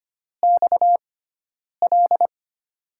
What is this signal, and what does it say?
Morse code, heard as a keyed tone, about 25 wpm, spaced as for 9 wpm, 700 Hz. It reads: XL
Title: USA Call Signs - 6x Character Spacing - 25wpm